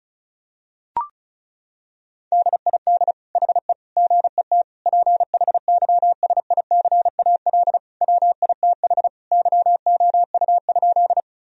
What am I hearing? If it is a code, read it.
Morse code, heard as a keyed tone, about 34 wpm, 700 Hz. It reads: DID HE GET PHYSICAL WITH YOU?
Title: Sentences from Top 1000 Words - Review Part 4 - 35wpm